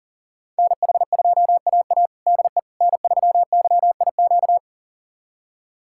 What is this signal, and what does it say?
Morse code, heard as a keyed tone, about 40 wpm, 700 Hz. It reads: NH2AA DE N3YIQ
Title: USA Callsign DE Callsign - 5 Characters - 40wpm